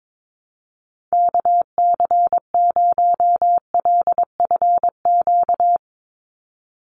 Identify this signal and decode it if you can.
Morse code, heard as a keyed tone, about 22 wpm, 700 Hz. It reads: KC0LFQ
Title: USA Call Signs 22wpm